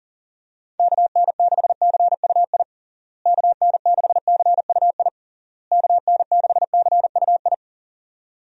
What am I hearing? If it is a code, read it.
Morse code, heard as a keyed tone, about 40 wpm, 700 Hz. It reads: KN6CUI KN6CUI KN6CUI
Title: USA Call Signs - 6 Characters - Repeated 3x - 40wpm